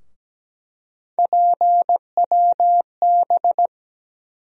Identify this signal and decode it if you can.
Morse code, heard as a keyed tone, about 17 wpm, 700 Hz. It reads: PWB
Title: USA Call Sign Postfixes - 3 Letters 17wpm